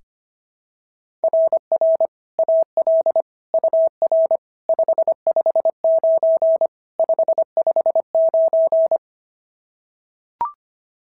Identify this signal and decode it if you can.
Morse code, heard as a keyed tone, about 22 wpm, 650 Hz. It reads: RR AL UR 559 559
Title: SOTA QSO Practice 25wpm